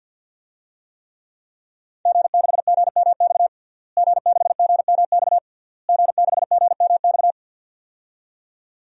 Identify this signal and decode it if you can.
Morse code, heard as a keyed tone, about 51 wpm, 700 Hz. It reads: G6GMX G6GMX G6GMX
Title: DXCC-100 Call Signs - 5 Characters - Repeated 3x - 50wpm